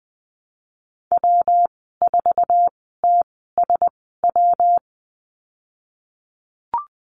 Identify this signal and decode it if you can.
Morse code, heard as a keyed tone, about 20 wpm, spaced as for 14 wpm, 700 Hz. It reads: W4TSW
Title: 173 Lesson 28g - Number 9 - Call Signs 20wpm